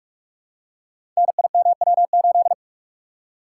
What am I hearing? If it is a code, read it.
Morse code, heard as a keyed tone, about 44 wpm, 700 Hz. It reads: NIMW8
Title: Code Groups - 5 Letter:Numbers - 45wpm